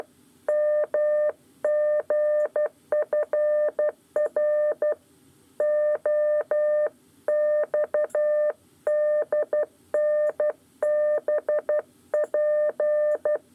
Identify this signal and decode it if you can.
Morse code, heard as a keyed tone, about 12 wpm, 600 Hz. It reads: MGFRO XDNBP
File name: 8 lezione cw 1